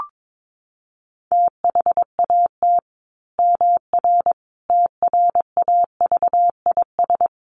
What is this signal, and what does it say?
Morse code, heard as a keyed tone, about 22 wpm, 700 Hz. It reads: THAT MR TRAVIS
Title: A Sound Of Thunder 22wpm